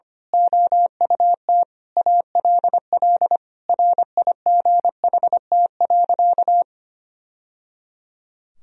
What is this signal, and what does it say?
Morse code, heard as a keyed tone, about 25 wpm, 700 Hz. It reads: OUT ALL RIGHT.
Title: Sentences from Top 1000 Words - Review Part 4 - 25wpm